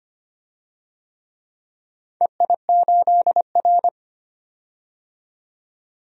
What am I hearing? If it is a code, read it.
Morse code, heard as a keyed tone, about 25 wpm, 700 Hz. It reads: EI8R
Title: DXCC-100 Call Signs - 4 Characters - 25wpm